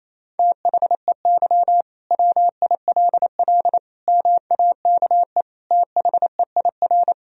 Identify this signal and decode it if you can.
Morse code, heard as a keyed tone, about 27 wpm, 700 Hz. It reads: THEY WILL MAKE THEIR
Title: Sentences from Top 1000 Words - Review Part 3 - 28wpm